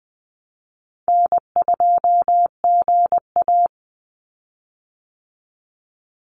Morse code, heard as a keyed tone, about 20 wpm, 700 Hz. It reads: N2GA
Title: Easy Call Signs Repeated 3x - 20wpm